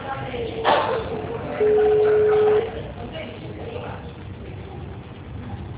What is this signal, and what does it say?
Ringback tone (Europe, 425 Hz), with speech.